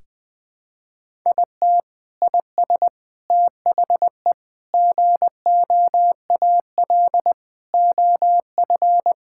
Morse code, heard as a keyed tone, about 20 wpm, 700 Hz. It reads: IT IS THE GOAL OF
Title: Sentences from Top 1000 Words - Review Part 5 - 20wpm